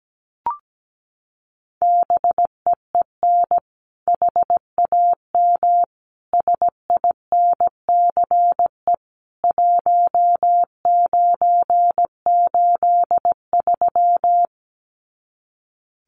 Morse code, heard as a keyed tone, about 17 wpm, 700 Hz. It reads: BEEN HAM SINCE 1983
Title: Common QSO Phrases 17wpm